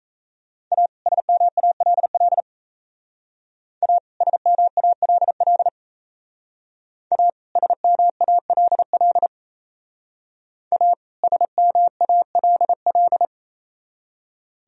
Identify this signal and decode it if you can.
Morse code, heard as a keyed tone, about 36 wpm, 700 Hz. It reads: A SMALL A SMALL A SMALL A SMALL